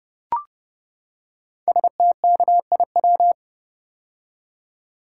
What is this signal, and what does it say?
Morse code, heard as a keyed tone, about 29 wpm, 700 Hz. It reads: STKIW
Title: Code Groups - 5 Letters - 30wpm